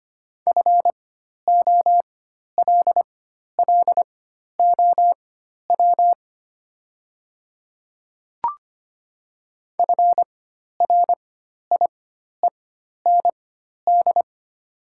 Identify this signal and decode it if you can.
Morse code, heard as a keyed tone, about 25 wpm, spaced as for 12 wpm, 690 Hz. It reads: FOLLOW FRIEND
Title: 4x Character Spacing - Top 300 Words - 25wpm